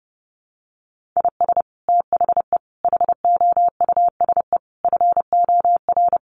Morse code, heard as a keyed tone, about 30 wpm, 700 Hz. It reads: IS THE HOUSE FOR
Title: Sentences from Top 1400 Words 30wpm